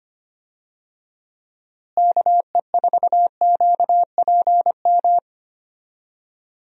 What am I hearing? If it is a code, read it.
Morse code, heard as a keyed tone, about 25 wpm, 700 Hz. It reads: KE4QPM
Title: USA Call Signs - 6 Characters - Repeated 3x - 25wpm